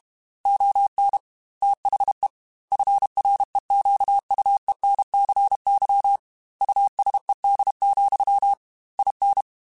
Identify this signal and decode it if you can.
Morse code, heard as a keyed tone, about 32 wpm, 800 Hz. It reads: ON THE FREQUENCY USED, IN
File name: Technical Subjects / Technical Subjects-mp3s / Sw Pwr Sup @30